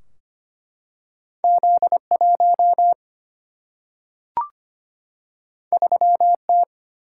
Morse code, heard as a keyed tone, about 25 wpm, 700 Hz. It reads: Z1 3T
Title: Single-Letter Single-Number 25wpm